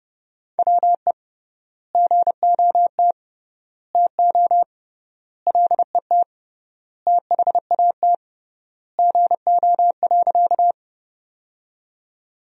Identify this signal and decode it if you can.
Morse code, heard as a keyed tone, about 30 wpm, 700 Hz. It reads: WE GOT TO LET THAT GO.
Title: 3x Word-Spacing - Sentences from Top 300 Words - 30wpm